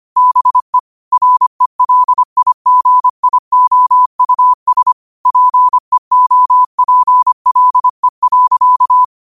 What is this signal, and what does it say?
Morse code, heard as a keyed tone, about 25 wpm, 1000 Hz. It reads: BE RELIGIOUS PEOPLE.